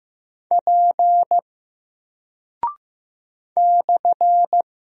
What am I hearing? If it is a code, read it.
Morse code, heard as a keyed tone, about 15 wpm, 700 Hz. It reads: P /
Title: Single Character (Letter:Number:Punctuation:Prosign) Alt 15wpm